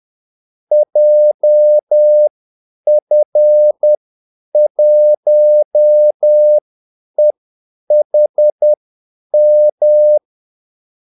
Morse code, heard as a keyed tone, about 10 wpm, spaced as for 8 wpm, 600 Hz. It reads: JF1EHM
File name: Callsigns-10-8wpm